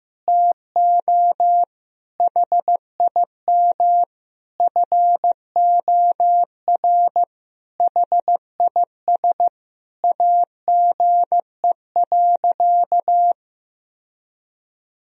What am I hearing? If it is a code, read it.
Morse code, heard as a keyed tone, about 15 wpm, 700 Hz. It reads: TO HIM FOR HIS AGE.
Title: Sentences from Top 1000 Words - Review Part 3 - 15wpm